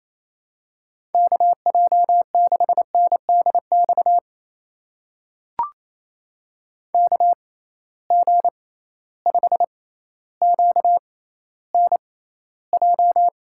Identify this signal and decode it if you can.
Morse code, heard as a keyed tone, about 27 wpm, spaced as for 14 wpm, 700 Hz. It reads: KJ6NDX KG5QNJ
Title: USA Call Signs - 6 Characters - 6x Character Spacing - 28wpm